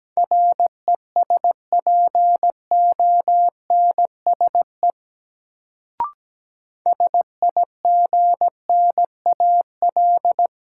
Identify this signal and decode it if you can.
Morse code, heard as a keyed tone, about 17 wpm, 700 Hz. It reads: RESPONSE SIGNAL
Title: Sentences from Top 1400 Words - New Word Drill Alt - 17wpm